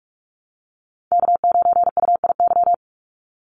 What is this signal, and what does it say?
Morse code, heard as a keyed tone, about 44 wpm, 700 Hz. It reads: K9UIY
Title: Call Signs Repeated 3x - 45wpm